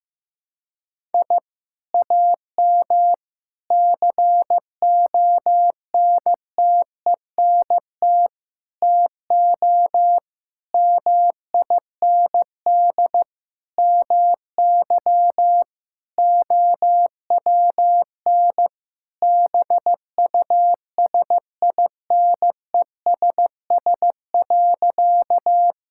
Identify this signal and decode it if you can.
Morse code, heard as a keyed tone, about 15 wpm, 700 Hz. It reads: I AM CONTENT TO MIND MY OWN BUSINESS.
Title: Sentences from Top 1000 Words - Review Part 6 - 15wpm